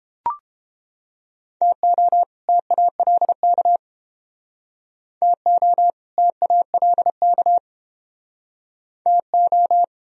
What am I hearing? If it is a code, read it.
Morse code, heard as a keyed tone, about 30 wpm, 700 Hz. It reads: TO TALK TO TALK TO